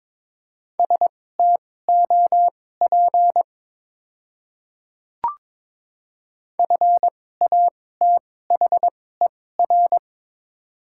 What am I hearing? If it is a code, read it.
Morse code, heard as a keyed tone, about 22 wpm, spaced as for 16 wpm, 700 Hz. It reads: STOP FATHER
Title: 2x Character Spacing - Top 300 Words - 22wpm